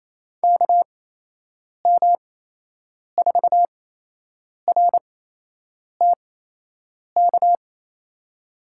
Morse code, heard as a keyed tone, about 28 wpm, spaced as for 8 wpm, 700 Hz. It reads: KM4RTK
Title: USA Call Signs - 6 Characters - 8x Character Spacing - 28wpm